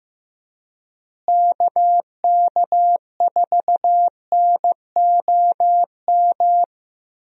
Morse code, heard as a keyed tone, about 15 wpm, 700 Hz. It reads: KK4NOM
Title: USA Call Signs - 6 Characters - Repeated 3x - 15wpm